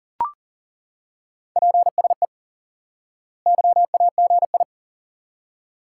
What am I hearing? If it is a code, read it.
Morse code, heard as a keyed tone, about 40 wpm, 700 Hz. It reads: PSE YAGI
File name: IQE Course 3 - Lesson 89 - YAGI - Two Elements - 40wpm